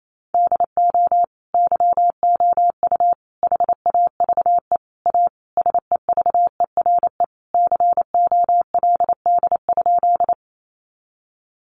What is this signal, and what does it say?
Morse code, heard as a keyed tone, about 28 wpm, 700 Hz. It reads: DO YOU HAVE A SEVERE COLD?
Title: Sentences from Top 1400 Words 28wpm